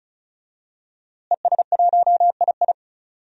Morse code, heard as a keyed tone, about 35 wpm, 700 Hz. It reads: ES1II
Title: DXCC-100 Call Signs - 5 Characters - Repeated 3x - 35wpm